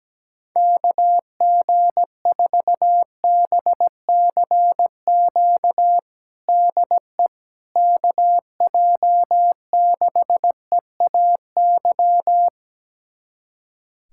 Morse code, heard as a keyed tone, about 17 wpm, 700 Hz. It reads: KG4BCQ DE KJ6EAY
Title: USA Callsign DE Callsign - 6 Characters - 17wpm